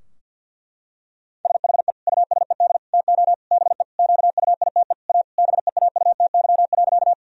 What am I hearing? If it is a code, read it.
Morse code, heard as a keyed tone, about 48 wpm, 700 Hz. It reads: SHE USED TO BE QUITE A BEAUTY.
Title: Sentences from Top 1000 Words - Review Part 2 - 50wpm